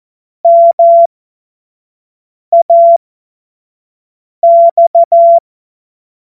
Morse code, heard as a keyed tone, about 14 wpm, spaced as for 5 wpm, 680 Hz. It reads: MAX
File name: Morsekurs / Witze / Witz 18. Januar 2022, 25 F70